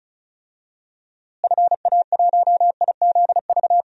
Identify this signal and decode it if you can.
Morse code, heard as a keyed tone, about 34 wpm, 700 Hz. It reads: FA1IZV
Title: DXCC-100 Call Signs - 6 Characters - Repeated 3x - 35wpm